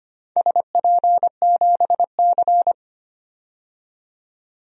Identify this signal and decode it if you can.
Morse code, heard as a keyed tone, about 25 wpm, 700 Hz. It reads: SP7C
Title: DXCC-100 Call Signs - 4 Characters - 25wpm